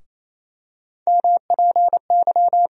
Morse code, heard as a keyed tone, about 28 wpm, 700 Hz. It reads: MPY
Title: USA Call Sign Postfixes - 3 Letters 28wpm